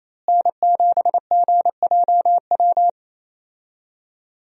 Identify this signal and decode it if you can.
Morse code, heard as a keyed tone, about 28 wpm, 700 Hz. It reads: N7GJW
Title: USA Call Signs - 5 Characters - Repeated 2x - 28wpm